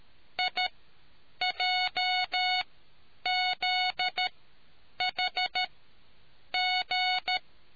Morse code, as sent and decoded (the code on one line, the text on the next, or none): .. .--- --.. .... --.
IJZHG